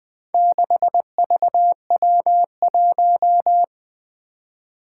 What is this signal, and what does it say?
Morse code, heard as a keyed tone, about 20 wpm, 700 Hz. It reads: 6VW1
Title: Four Challenging Letter-Number Combinations 20wpm